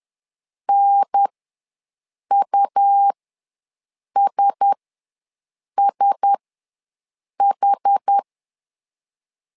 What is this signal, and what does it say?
Morse code, heard as a keyed tone, about 10 wpm, spaced as for 6 wpm, 800 Hz. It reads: NUSSH
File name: LEZIONI / 4 Q1S ws 10wpm 2208